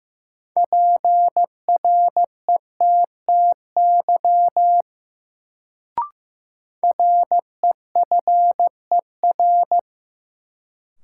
Morse code, heard as a keyed tone, about 15 wpm, 700 Hz. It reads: PRETTY REFER
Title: Sentences from Top 1000 Words - New Word Drill - 15wpm